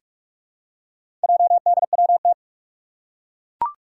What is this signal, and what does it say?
Morse code, heard as a keyed tone, about 42 wpm, 700 Hz. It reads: JDWT
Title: Four Challenging Letter-Number Combinations 45wpm